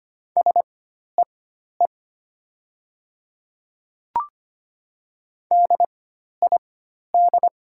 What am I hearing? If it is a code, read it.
Morse code, heard as a keyed tone, about 25 wpm, spaced as for 12 wpm, 700 Hz. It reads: SEE DID
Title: 4x Character Spacing - Top 100 Words - 25wpm